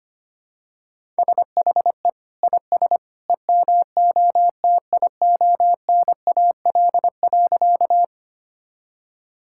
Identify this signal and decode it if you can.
Morse code, heard as a keyed tone, about 25 wpm, 700 Hz. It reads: SHE IS EMOTIONAL.
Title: Sentences from Top 1000 Words - Review Part 5 - 25wpm